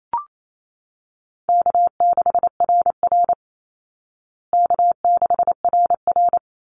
Morse code, heard as a keyed tone, about 28 wpm, 700 Hz. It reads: K6RR K6RR
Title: Call Signs Repeated 3x - 28wpm